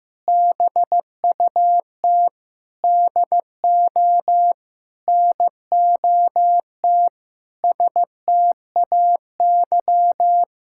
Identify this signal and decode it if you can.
Morse code, heard as a keyed tone, about 16 wpm, 700 Hz. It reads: BUT DO NOT STAY
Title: Sentences from Top 1600 Words 15wpm